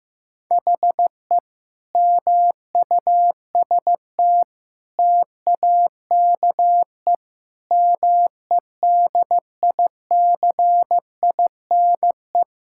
Morse code, heard as a keyed tone, about 15 wpm, 700 Hz. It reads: HE MUST TAKE MEDICINE
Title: Sentences from Top 1000 Words - Review Part 6 - 15wpm